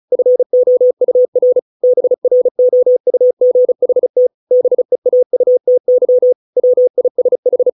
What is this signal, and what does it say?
Morse code, heard as a keyed tone, about 34 wpm, 500 Hz. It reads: FOUR BROUGHT BEAUTY WISH